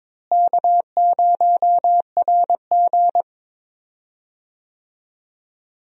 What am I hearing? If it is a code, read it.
Morse code, heard as a keyed tone, about 22 wpm, 700 Hz. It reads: K0RG
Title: USA Call Signs - 4 Characters - Repeated 2x - 22wpm